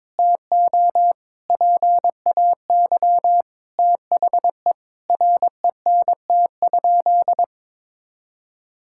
Morse code, heard as a keyed tone, about 22 wpm, 700 Hz. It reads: TO PAY THE RENT?
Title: Sentences from Top 1000 Words - Review Part 5 - 22wpm